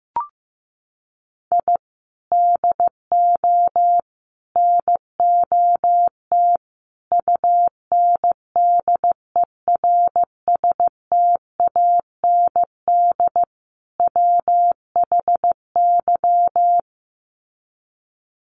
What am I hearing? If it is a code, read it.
Morse code, heard as a keyed tone, about 15 wpm, 700 Hz. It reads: I DO NOT UNDERSTAND WHY
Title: Sets of 5 Words Encore 15wpm